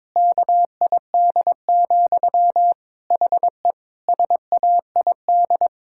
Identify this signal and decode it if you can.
Morse code, heard as a keyed tone, about 22 wpm, 700 Hz. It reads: KID, HE SAID
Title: Sentences from Top 1600 Words 22wpm